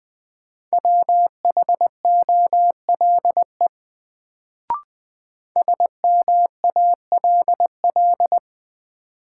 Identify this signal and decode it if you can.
Morse code, heard as a keyed tone, about 21 wpm, 700 Hz. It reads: WHOLE SMALL
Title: Sets of 2 Words - New Word Drill - 20wpm